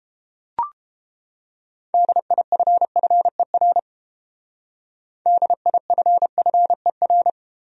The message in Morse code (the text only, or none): DIFFER DIFFER